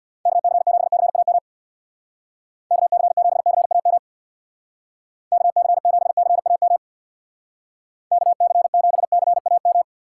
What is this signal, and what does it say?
Morse code, heard as a keyed tone, about 63 wpm, 700 Hz. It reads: KC7CAM KC7CAM KC7CAM KC7CAM